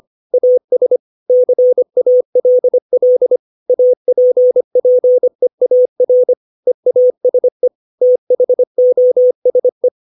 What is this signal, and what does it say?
Morse code, heard as a keyed tone, about 25 wpm, 500 Hz. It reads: AS CALL APPEAR EASE THOSE
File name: 632 words / 632 Words 25WPM